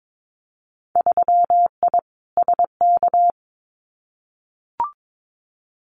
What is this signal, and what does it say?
Morse code, heard as a keyed tone, about 22 wpm, 700 Hz. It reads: 3I SK
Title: Winter Field Day Exchanges Encore 22wpm